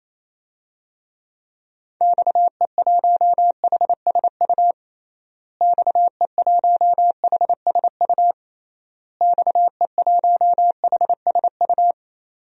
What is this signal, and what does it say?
Morse code, heard as a keyed tone, about 27 wpm, 700 Hz. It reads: XE1HSU XE1HSU XE1HSU
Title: DXCC-100 Call Signs - 6 Characters - Repeated 3x - 28wpm